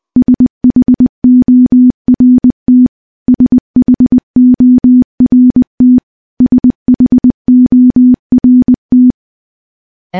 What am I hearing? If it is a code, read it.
Morse code, heard as a keyed tone, about 20 wpm, 250 Hz. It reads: SHORT SHORT SHORT
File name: morse-training-20WPM